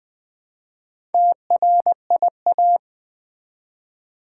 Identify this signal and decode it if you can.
Morse code, heard as a keyed tone, about 20 wpm, 700 Hz. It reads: TRIA
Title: Build A Word - 1 Letter At A Time - 5-Letter Words - 20wpm